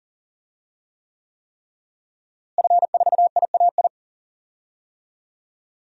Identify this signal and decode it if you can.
Morse code, heard as a keyed tone, about 40 wpm, 700 Hz. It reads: F4IAI
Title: DXCC-100 Call Signs - 5 Characters - Repeated 3x - 40wpm